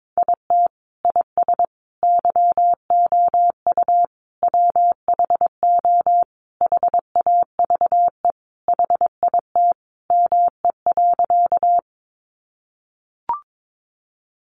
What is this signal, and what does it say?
Morse code, heard as a keyed tone, about 22 wpm, 700 Hz. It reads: IT IS YOU WHO HAVE HIT ME.
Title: Sentences from Top 1200 Words 22wpm